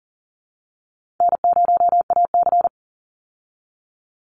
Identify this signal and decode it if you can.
Morse code, heard as a keyed tone, about 40 wpm, 700 Hz. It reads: N0AC
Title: Call Signs Repeated 3x - 40wpm